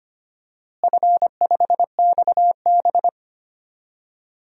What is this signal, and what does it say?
Morse code, heard as a keyed tone, about 25 wpm, 700 Hz. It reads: F5XB
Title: DXCC-100 Call Signs - 4 Characters - Repeated 3x - 25wpm